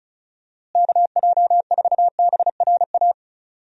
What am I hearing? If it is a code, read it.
Morse code, heard as a keyed tone, about 36 wpm, 700 Hz. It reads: KJ4BRA
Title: USA Call Signs - 6 Characters - Repeated 3x - 35wpm